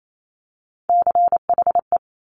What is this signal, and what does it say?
Morse code, heard as a keyed tone, about 28 wpm, 690 Hz. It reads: CHE
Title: Top Three Letter Combinations in Words 28wpm